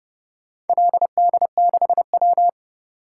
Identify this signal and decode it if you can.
Morse code, heard as a keyed tone, about 29 wpm, 700 Hz. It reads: LD6W